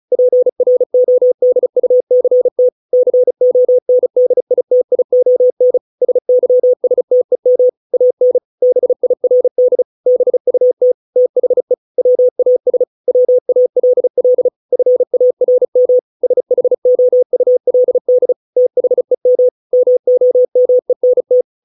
Morse code, heard as a keyed tone, about 34 wpm, 500 Hz. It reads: PRODUCT CONDITION SYSTEM AN BIRD BUT THE WAS WALL FARM SHOULD THEM MOMENT